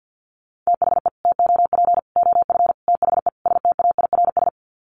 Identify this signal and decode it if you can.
Morse code, heard as a keyed tone, about 50 wpm, 700 Hz. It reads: THE TOP OF THE STAIRS
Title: Sets of 5 Words Encore 50wpm